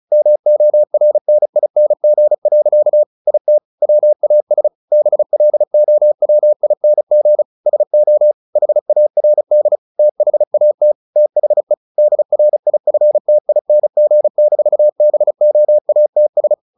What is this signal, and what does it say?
Morse code, heard as a keyed tone, about 35 wpm, 600 Hz. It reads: MORNING. IT WAS BLOWING SO HARD THAT THE DRIFTING-BOATS
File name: the old man and the sea 2 35wpm